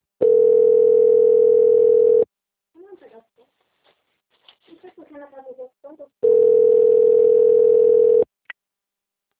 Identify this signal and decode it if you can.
Ringback tone (North America), 440 + 480 Hz, with speech.
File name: out- - - - - .53534